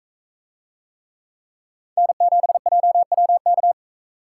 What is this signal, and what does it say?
Morse code, heard as a keyed tone, about 42 wpm, 700 Hz. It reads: N7JWK